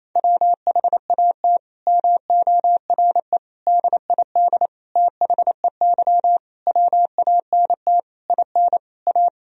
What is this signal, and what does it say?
Morse code, heard as a keyed tone, about 28 wpm, 700 Hz. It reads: WHAT MORE DID THEY WANT IN A
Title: Sentences from Top 1000 Words - Review Part 3 - 28wpm